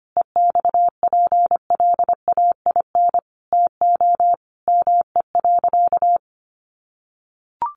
Morse code, heard as a keyed tone, about 25 wpm, 700 Hz. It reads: EXPLAIN TO ME.
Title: Sentences from Top 1100 Words 25wpm